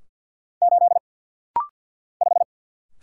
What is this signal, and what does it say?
Morse code, heard as a keyed tone, about 50 wpm, 700 Hz. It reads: 8 5
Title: Single Letter-Number - Rapid-Fire 50wpm